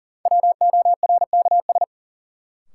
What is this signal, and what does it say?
Morse code, heard as a keyed tone, about 40 wpm, 700 Hz. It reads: WORKS
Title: Sentences from Top 1000 Words - Review Part 5 - New Word Drill - 40wpm